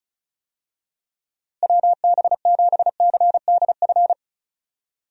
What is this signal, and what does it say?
Morse code, heard as a keyed tone, about 34 wpm, 700 Hz. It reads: WB7CDF
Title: USA Call Signs - 6 Characters - Repeated 3x - 35wpm